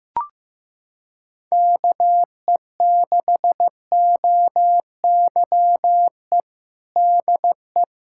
Morse code, heard as a keyed tone, about 16 wpm, 700 Hz. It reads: KE6OYE DE
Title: Practice QSO - Exchange 4 - Ending - 15wpm